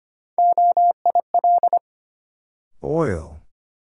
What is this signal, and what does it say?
Morse code, heard as a keyed tone, about 25 wpm, 700 Hz. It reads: OIL
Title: Sentences from Top 1000 Words - Review Part 4 - New Word Drill - 25wpm